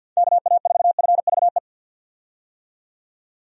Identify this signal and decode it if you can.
Morse code, heard as a keyed tone, about 52 wpm, 700 Hz. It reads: KA4FVE
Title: USA Call Signs - 6 Characters - 50wpm